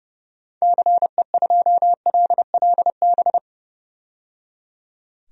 Morse code, heard as a keyed tone, about 30 wpm, 700 Hz. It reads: CE2LLB
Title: DXCC-100 Call Signs - 6 Characters - 30wpm